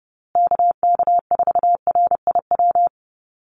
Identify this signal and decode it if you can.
Morse code, heard as a keyed tone, about 30 wpm, 700 Hz. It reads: KK4RIW
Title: USA Call Signs Repeated 3x 30wpm